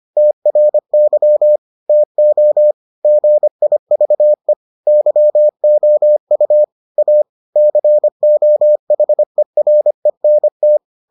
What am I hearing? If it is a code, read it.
Morse code, heard as a keyed tone, about 25 wpm, 600 Hz. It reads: TRY TO GIVE YOU A COHERENT